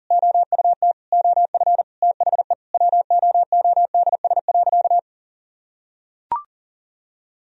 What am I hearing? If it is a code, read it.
Morse code, heard as a keyed tone, about 37 wpm, 700 Hz. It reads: OUT OF THE WOODS.
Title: Sentences from Top 1000 Words - Review Part 2 - 40wpm